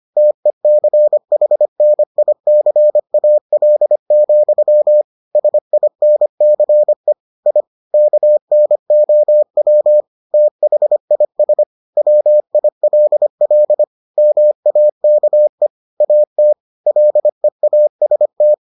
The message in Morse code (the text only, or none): TECHNICAL, SINCE I KNOW THIS WILL MAKE AT LEAST